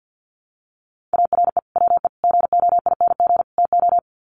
Morse code, heard as a keyed tone, about 48 wpm, 700 Hz. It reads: ARE WE GOING TO
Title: Sets of 4 Words 50wpm